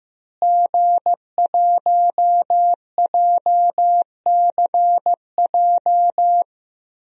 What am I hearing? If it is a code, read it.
Morse code, heard as a keyed tone, about 16 wpm, 700 Hz. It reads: G1JCJ
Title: DXCC-100 Call Signs - 5 Characters - Repeated 3x - 15wpm